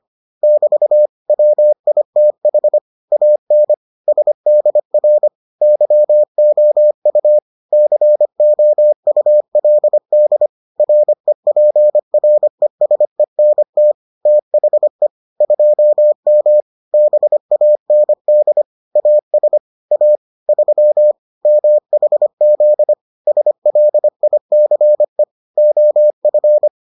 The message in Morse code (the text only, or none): <BT> WITH AN SDR YOU COULD REPRESENT THE 2M BAND AS A 3 MHZ SLICE OF